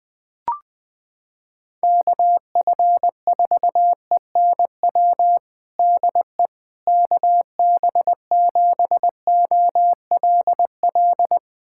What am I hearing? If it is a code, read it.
Morse code, heard as a keyed tone, about 20 wpm, 700 Hz. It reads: KF4ENW DE KB7OLL
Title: USA Callsign DE Callsign - 6 Characters - 20wpm